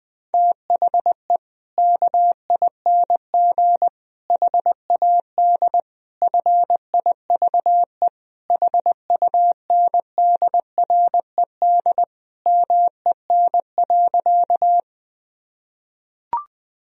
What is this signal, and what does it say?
Morse code, heard as a keyed tone, about 20 wpm, 700 Hz. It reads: THE KING HAD FIVE HUNDRED MEN.
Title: Sentences from Top 1000 Words - Review Part 2 - 20wpm